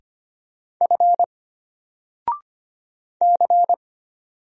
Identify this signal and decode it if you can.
Morse code, heard as a keyed tone, about 25 wpm, 700 Hz. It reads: F C
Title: Single Letter-Number 25wpm